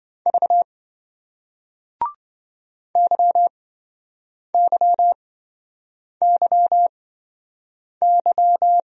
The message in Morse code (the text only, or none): V YYYY